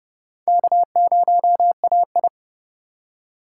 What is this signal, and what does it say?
Morse code, heard as a keyed tone, about 30 wpm, 700 Hz. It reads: K0AI
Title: USA Call Signs - 3x Character Spacing - 30wpm